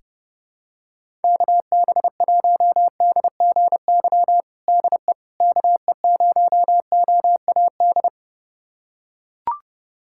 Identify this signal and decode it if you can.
Morse code, heard as a keyed tone, about 30 wpm, 700 Hz. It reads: KB1DGY DE KE0OAD
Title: USA Callsign DE Callsign - 6 Characters - 30wpm